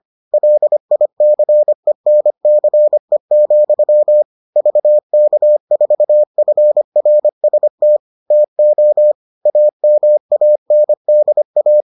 Morse code, heard as a keyed tone, about 25 wpm, 600 Hz. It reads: LICENCE, VK4FRST TO AMANDA